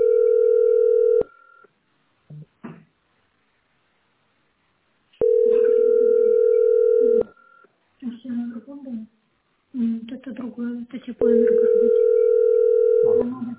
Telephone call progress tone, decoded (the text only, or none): ringback tone